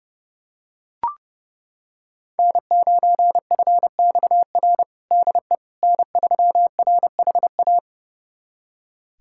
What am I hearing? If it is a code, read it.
Morse code, heard as a keyed tone, about 30 wpm, 700 Hz. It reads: N9FXR DE N3RHA
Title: USA Callsign DE Callsign - 5 Characters - 30wpm